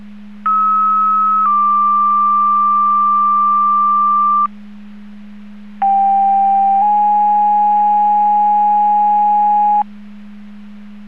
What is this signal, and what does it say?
Two-tone page, heard as a one-second tone, then a three-second tone, about 1290 then 1180 Hz, about 790 then 830 Hz. It is heard from 0.5 to 9.8 s.